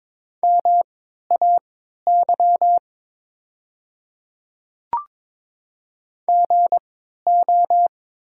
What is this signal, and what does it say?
Morse code, heard as a keyed tone, about 23 wpm, spaced as for 13 wpm, 700 Hz. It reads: MAY GO
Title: 3x Character Spacing - Top 100 Words - 22wpm